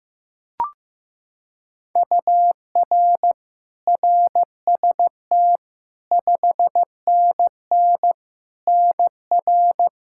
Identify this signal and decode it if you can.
Morse code, heard as a keyed tone, about 15 wpm, 700 Hz. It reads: UR RST 5NN NR
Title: Practice QSO - Exchange 1 - Beginning - 15wpm